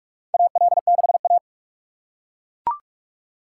Morse code, heard as a keyed tone, about 41 wpm, 690 Hz. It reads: AL6A
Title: USA Call Signs - 4 Characters - 8x Character Spacing - 45wpm